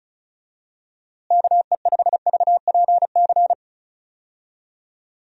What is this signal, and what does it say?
Morse code, heard as a keyed tone, about 34 wpm, 700 Hz. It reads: KE5VPC
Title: USA Call Signs - 6 Characters - Repeated 3x - 35wpm